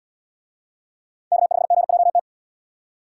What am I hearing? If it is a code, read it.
Morse code, heard as a keyed tone, about 77 wpm, 700 Hz. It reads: K5GJT